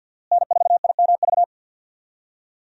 Morse code, heard as a keyed tone, about 52 wpm, 700 Hz. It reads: N4IMV
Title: USA Call Signs - 5 Characters - Repeated 2x - 50wpm